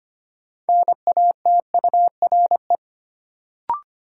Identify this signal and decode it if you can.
Morse code, heard as a keyed tone, about 25 wpm, 700 Hz. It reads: NATURE
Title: Sentences from Top 600 Words - New Word Drill - 25wpm